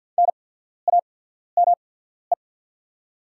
Morse code, heard as a keyed tone, about 48 wpm, spaced as for 14 wpm, 700 Hz. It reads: NAME
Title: 8x Character Spacing - Top 200 Words - 50wpm